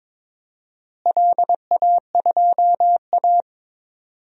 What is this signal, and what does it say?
Morse code, heard as a keyed tone, about 22 wpm, 700 Hz. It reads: LA2A
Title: DXCC-100 Call Signs - 4 Characters - Repeated 2x - 22wpm